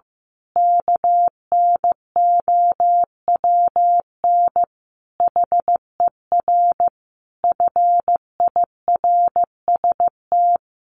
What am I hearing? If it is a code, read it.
Morse code, heard as a keyed tone, about 15 wpm, 700 Hz. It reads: KNOWN HER FIRST
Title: Sentences from Top 1200 Words 15wpm